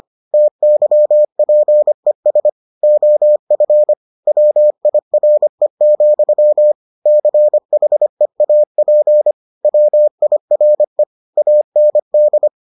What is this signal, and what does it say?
Morse code, heard as a keyed tone, about 25 wpm, 600 Hz. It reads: TYPES OF WIRE, CHEAP WIRE AND